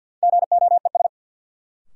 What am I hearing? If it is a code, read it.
Morse code, heard as a keyed tone, about 47 wpm, 700 Hz. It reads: GOES
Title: Sentences from Top 1000 Words - Review Part 5 - New Word Drill - 50wpm